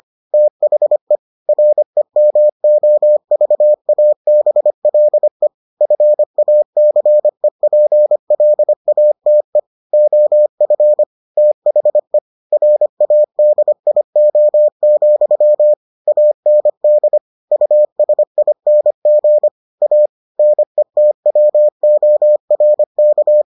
Morse code, heard as a keyed tone, about 25 wpm, 600 Hz. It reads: THE REMOVABLE FACEPLATE OF THE RADIO, AND USING A NETWORK